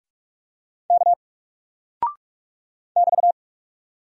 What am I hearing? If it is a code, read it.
Morse code, heard as a keyed tone, about 44 wpm, 700 Hz. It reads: K <BT>
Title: Single Character (Letter:Number:Punctuation:Prosign) 45wpm